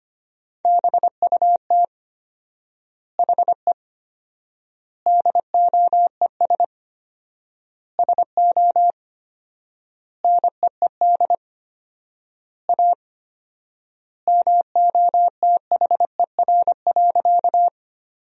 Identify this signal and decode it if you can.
Morse code, heard as a keyed tone, about 25 wpm, 700 Hz. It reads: BUT HE DOES SO NEED A MOTHER.
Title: 4x Word-Spacing - Sentences from Top 200 Words - 25wpm